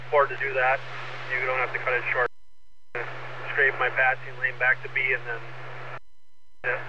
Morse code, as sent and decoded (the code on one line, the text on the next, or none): none